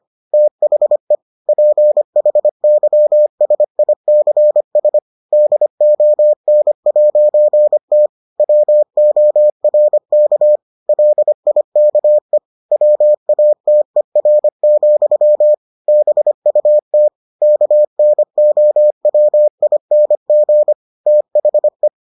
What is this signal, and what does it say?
Morse code, heard as a keyed tone, about 25 wpm, 600 Hz. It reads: THE PHYSICS DON'T WORK LIKE WATER, BUT KNOWING THE